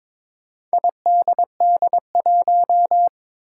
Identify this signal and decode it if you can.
Morse code, heard as a keyed tone, about 22 wpm, 700 Hz. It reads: IDD1
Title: Four Challenging Letter-Number Combinations 22wpm